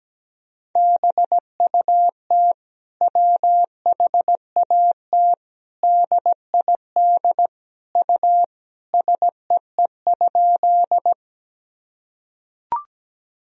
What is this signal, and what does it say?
Morse code, heard as a keyed tone, about 17 wpm, 700 Hz. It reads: BUT WHAT DID U SEE?
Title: Sentences from Top 100 Words and Abbreviations 17wpm